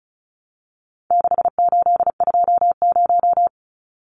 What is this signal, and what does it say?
Morse code, heard as a keyed tone, about 34 wpm, 700 Hz. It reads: 6820
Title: Numbers - 4-Digit 35wpm